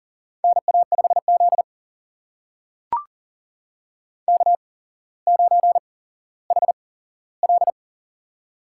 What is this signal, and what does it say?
Morse code, heard as a keyed tone, about 39 wpm, spaced as for 17 wpm, 700 Hz. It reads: NA5Z K9HL
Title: USA Call Signs - 4 Characters - 8x Character Spacing - 40wpm